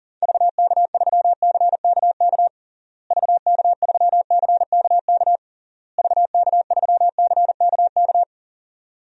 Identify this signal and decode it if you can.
Morse code, heard as a keyed tone, about 40 wpm, 700 Hz. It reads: VK3CKK VK3CKK VK3CKK
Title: DXCC-100 Call Signs - 6 Characters - Repeated 3x - 40wpm